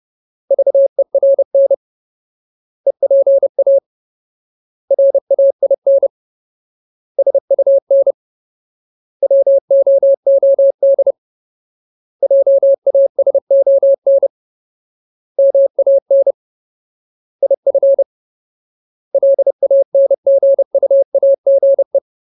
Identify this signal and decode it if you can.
Morse code, heard as a keyed tone, about 30 wpm, 550 Hz. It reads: VERN EPA RAIN SUN WOOD JASON MAN IF LANGUAGE
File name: Ultimate Word List / 30WPM Ultimate